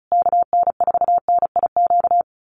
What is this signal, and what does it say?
Morse code, heard as a keyed tone, about 34 wpm, 700 Hz. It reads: KN4NIQ
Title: USA Call Signs Repeated 3x 35wpm